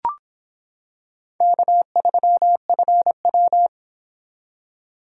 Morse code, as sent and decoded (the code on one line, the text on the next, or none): -.- ...-- ..-. .--
K3FW